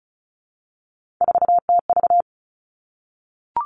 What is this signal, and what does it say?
Morse code, heard as a keyed tone, about 34 wpm, 700 Hz. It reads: 4TV
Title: Three Challenging Letter-Number Combinations 35wpm